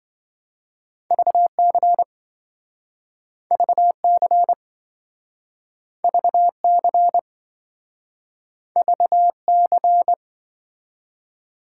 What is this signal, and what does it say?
Morse code, heard as a keyed tone, about 27 wpm, 700 Hz. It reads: VC VC VC VC